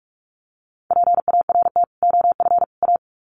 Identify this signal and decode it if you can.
Morse code, heard as a keyed tone, about 44 wpm, 700 Hz. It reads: PART OF A
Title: Sets of 3 Words 45wpm